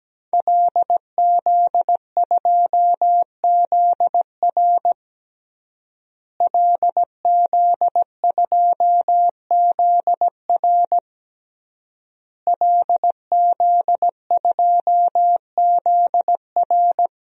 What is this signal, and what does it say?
Morse code, heard as a keyed tone, about 17 wpm, 700 Hz. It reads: LZ2ZR LZ2ZR LZ2ZR
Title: DXCC-100 Call Signs - 5 Characters - Repeated 3x - 17wpm